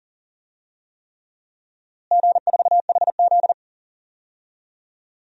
Morse code, heard as a keyed tone, about 40 wpm, 700 Hz. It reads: G4HZ
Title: DXCC-100 Call Signs - 4 Characters - Repeated 3x - 40wpm